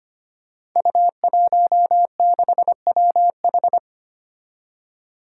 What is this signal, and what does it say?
Morse code, heard as a keyed tone, about 25 wpm, 700 Hz. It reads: U16WH
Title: Five Challenging Letter-Number Combinations 25wpm